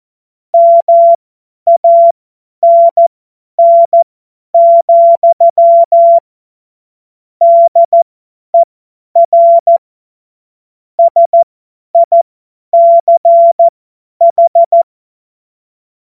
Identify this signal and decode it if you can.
Morse code, heard as a keyed tone, about 14 wpm, spaced as for 10 wpm, 680 Hz. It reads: MANN, DER SICH
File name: Morsekurs / Witze / Witz 02 Januar 2024 50 F70